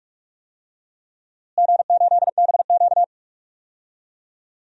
Morse code, heard as a keyed tone, about 46 wpm, 700 Hz. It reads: G8BQ
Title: DXCC-100 Call Signs - 4 Characters - Repeated 3x - 45wpm